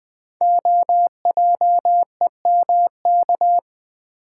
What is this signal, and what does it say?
Morse code, heard as a keyed tone, about 20 wpm, 700 Hz. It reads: OJEMK
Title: Code Groups - 5 Letters - 20wpm